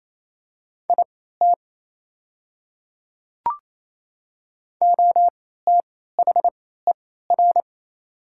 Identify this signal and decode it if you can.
Morse code, heard as a keyed tone, about 27 wpm, spaced as for 16 wpm, 700 Hz. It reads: IT OTHER
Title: 3x Character Spacing - Top 100 Words - 28wpm